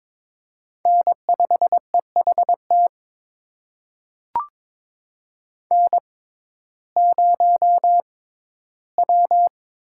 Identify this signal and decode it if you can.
Morse code, heard as a keyed tone, about 22 wpm, spaced as for 14 wpm, 700 Hz. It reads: N5EHT N0W
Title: USA Call Signs - 5 Characters - 6x Character Spacing - 22wpm